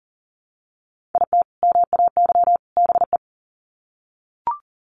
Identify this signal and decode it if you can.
Morse code, heard as a keyed tone, about 40 wpm, 700 Hz. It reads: IT MAY BE
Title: Sets of 3 Words 40wpm